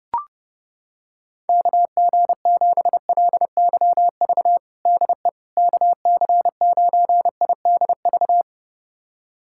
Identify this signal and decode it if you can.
Morse code, heard as a keyed tone, about 30 wpm, 700 Hz. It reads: KG7LYV DE KC9IDV
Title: USA Callsign DE Callsign - 6 Characters - 30wpm